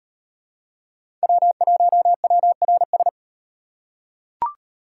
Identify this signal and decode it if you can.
Morse code, heard as a keyed tone, about 38 wpm, 700 Hz. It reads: W1WRS